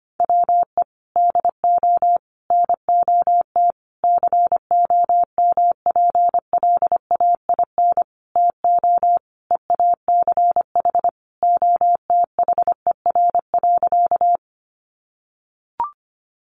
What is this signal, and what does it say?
Morse code, heard as a keyed tone, about 25 wpm, 700 Hz. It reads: WE DO NOT COMPLAIN TO EACH OTHER.
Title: Sentences from Top 1300 Words 25wpm